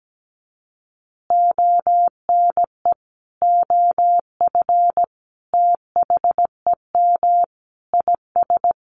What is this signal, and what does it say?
Morse code, heard as a keyed tone, about 17 wpm, 700 Hz. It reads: ONE OF THEM IS
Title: Additional Easy Phrases 17wpm